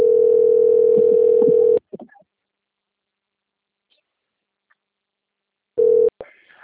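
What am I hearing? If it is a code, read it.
Ringback tone (North America), 440 + 480 Hz.
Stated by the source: Outgoing telephone call recording, Piracicaba.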